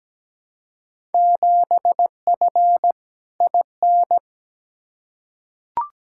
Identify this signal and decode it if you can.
Morse code, heard as a keyed tone, about 17 wpm, 700 Hz. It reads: 7F IN
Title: Field Day Exchanges Encore - 17wpm